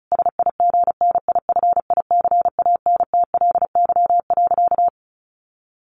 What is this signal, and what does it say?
Morse code, heard as a keyed tone, about 34 wpm, 700 Hz. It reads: SIGNIFICANTLY.
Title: Sentences from Top 1500 Words 35wpm